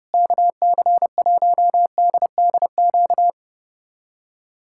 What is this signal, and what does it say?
Morse code, heard as a keyed tone, about 30 wpm, 700 Hz. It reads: KC1DDQ
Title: USA Callsign DE Callsign - 6 Characters - 30wpm